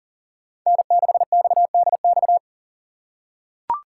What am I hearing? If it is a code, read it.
Morse code, heard as a keyed tone, about 40 wpm, 700 Hz. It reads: N6XDX
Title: USA Call Signs - 5 Characters - 8x Character Spacing - 40wpm